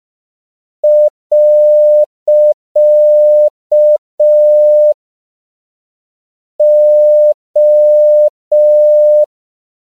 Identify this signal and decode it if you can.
Morse code, heard as a keyed tone, about 5 wpm, 600 Hz. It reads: . O
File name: CW / 5PPM